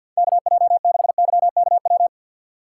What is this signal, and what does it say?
Morse code, heard as a keyed tone, about 52 wpm, 690 Hz. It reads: KJ6YKW
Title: USA Call Signs - 2x Character Spacing - 50wpm